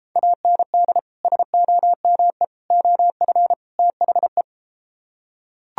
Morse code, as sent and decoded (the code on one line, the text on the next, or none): .- -. -.. / ... --- -- . / --- ..-. / - .... .
AND SOME OF THE